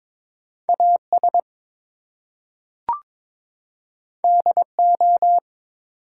Morse code, heard as a keyed tone, about 22 wpm, 700 Hz. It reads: AS DO
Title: Two Letter Words - 22wpm